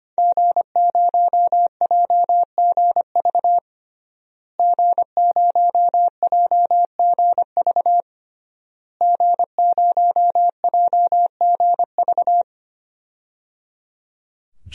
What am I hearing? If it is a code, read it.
Morse code, heard as a keyed tone, about 25 wpm, 700 Hz. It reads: G0JGV G0JGV G0JGV
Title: DXCC-100 Call Signs - 5 Characters - Repeated 3x - 25wpm